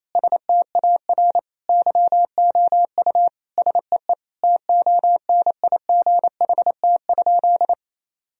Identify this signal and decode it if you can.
Morse code, heard as a keyed tone, about 28 wpm, 700 Hz. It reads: STAR YOU SEE TONIGHT?
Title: Sentences from Top 1000 Words - Review Part 2 - 28wpm